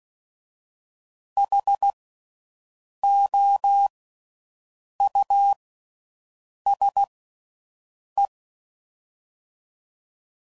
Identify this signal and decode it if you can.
Morse code, heard as a keyed tone, about 16 wpm, 800 Hz. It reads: H O U S E
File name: Suffixes / General / Houses